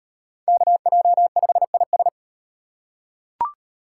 Morse code, heard as a keyed tone, about 37 wpm, 700 Hz. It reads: KJ5IS